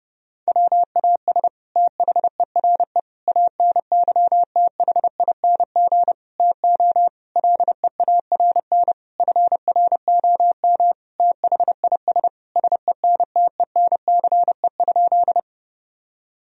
Morse code, heard as a keyed tone, about 30 wpm, 700 Hz. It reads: WAS THERE ANYTHING TO LEARN FROM THIS SENTENCE?
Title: Sentences from Top 1000 Words - Review Part 1 - 30wpm